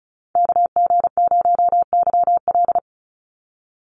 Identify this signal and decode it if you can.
Morse code, heard as a keyed tone, about 34 wpm, 700 Hz. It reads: KG0YL
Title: Call Signs - 35wpm